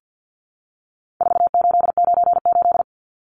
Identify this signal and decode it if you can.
Morse code, heard as a keyed tone, about 50 wpm, 700 Hz. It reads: 4898
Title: Numbers - 4-Digit 50wpm